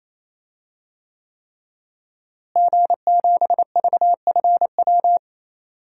Morse code, heard as a keyed tone, about 28 wpm, 700 Hz. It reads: G7VFW
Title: DXCC-100 Call Signs - 5 Characters - Repeated 3x - 28wpm